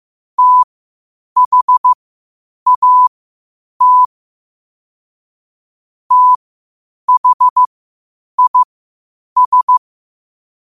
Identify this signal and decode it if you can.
Morse code, heard as a keyed tone, about 15 wpm, spaced as for 8 wpm, 1000 Hz. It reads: THAT THIS
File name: qotdfiles / 8wpm